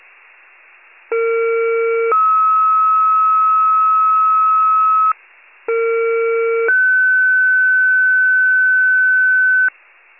Two-tone page, heard as a one-second tone, then a three-second tone, about 460 then 1290 Hz, about 460 then 1500 Hz. It is heard from 1.1 to 9.7 s.
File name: pager / 23 FIRE / wcalert